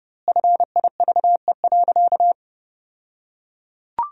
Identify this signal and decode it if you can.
Morse code, heard as a keyed tone, about 32 wpm, 700 Hz. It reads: FIVE.
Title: 2x Word-Spacing - Sentences from Top 400 Words - 30wpm